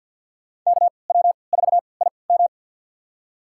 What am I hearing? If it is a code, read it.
Morse code, heard as a keyed tone, about 52 wpm, spaced as for 29 wpm, 700 Hz. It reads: KW4IM
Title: USA Call Signs - 3x Character Spacing - 50wpm